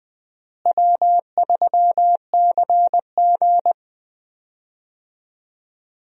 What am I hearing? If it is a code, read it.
Morse code, heard as a keyed tone, about 20 wpm, 700 Hz. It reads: W3CG
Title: USA Call Signs - 4 Characters - Repeated 3x - 20wpm